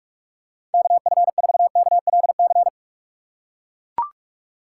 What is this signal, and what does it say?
Morse code, heard as a keyed tone, about 42 wpm, 700 Hz. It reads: KF4KLC
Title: USA Call Signs - 6 Characters - 3x Character Spacing - 45wpm